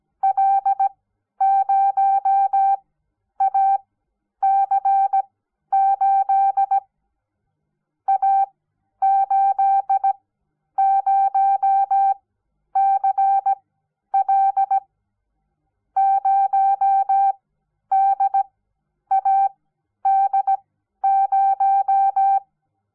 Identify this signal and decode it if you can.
Morse code, heard as a keyed tone, about 17 wpm, spaced as for 10 wpm, 780 Hz. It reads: L0AC8 A80CL 0DAD0
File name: Onlinelehrgang / morsen / DJ4UF Morsen06